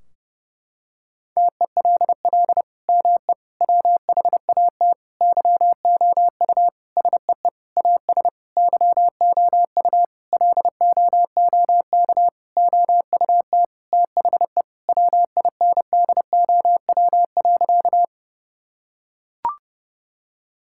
Morse code, heard as a keyed tone, about 30 wpm, 700 Hz. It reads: TELL ME WHAT YOU SEE AS YOU LOOK OUT THE WINDOW.
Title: Sentences from Top 1600 Words 30wpm